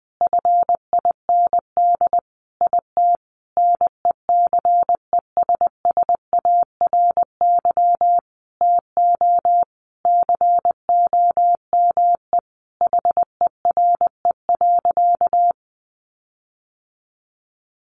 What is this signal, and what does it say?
Morse code, heard as a keyed tone, about 20 wpm, 700 Hz. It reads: FIND IT NECESSARY TO COME HERE.
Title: Sentences from Top 1300 Words 20wpm